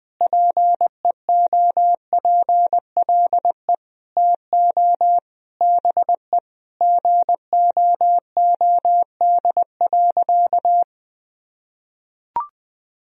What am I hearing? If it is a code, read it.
Morse code, heard as a keyed tone, about 20 wpm, 700 Hz. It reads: PEOPLE TO BE GOOD.
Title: Sentences from Top 1000 Words - Review Part 3 - 20wpm